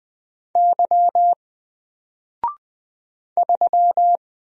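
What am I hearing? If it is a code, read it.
Morse code, heard as a keyed tone, about 20 wpm, 700 Hz. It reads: Y 3
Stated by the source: Single letter and number Morse drill at 20 wpm, from N4LCD.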